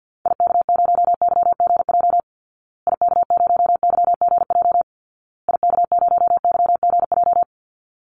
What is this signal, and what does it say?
Morse code, heard as a keyed tone, about 49 wpm, 700 Hz. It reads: IK0YGJ IK0YGJ IK0YGJ
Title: Call Signs Repeated 2x - 50wpm